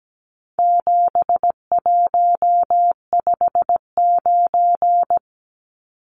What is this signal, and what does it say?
Morse code, heard as a keyed tone, about 17 wpm, 700 Hz. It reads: 7159
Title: Numbers - 4-Digit 17wpm